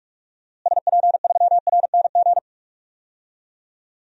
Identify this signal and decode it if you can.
Morse code, heard as a keyed tone, about 43 wpm, 700 Hz. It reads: SP3RNG
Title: DXCC-100 Call Signs - 6 Characters - Repeated 3x - 45wpm